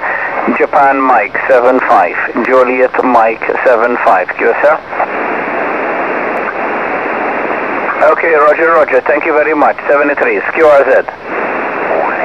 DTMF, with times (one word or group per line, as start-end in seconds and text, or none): none